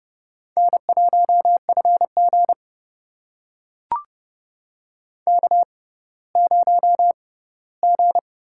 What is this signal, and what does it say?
Morse code, heard as a keyed tone, about 32 wpm, spaced as for 17 wpm, 700 Hz. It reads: N1FG K0G
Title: USA Call Signs - 4 Characters - 6x Character Spacing - 30wpm